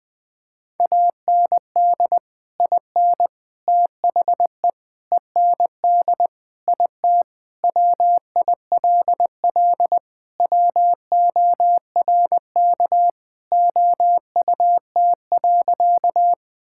Morse code, heard as a keyed tone, about 20 wpm, 700 Hz. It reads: AND IN THE END IT WILL WORK OUT.
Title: Sentences with 3-Word Phrases - 20wpm